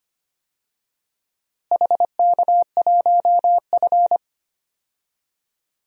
Morse code, heard as a keyed tone, about 25 wpm, 700 Hz. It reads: HK1F
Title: DXCC-100 Call Signs - 4 Characters - Repeated 3x - 25wpm